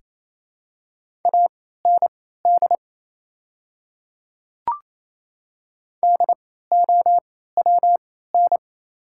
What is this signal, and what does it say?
Morse code, heard as a keyed tone, about 27 wpm, spaced as for 16 wpm, 690 Hz. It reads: AND DOWN
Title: 3x Character Spacing - Top 200 Words - 28wpm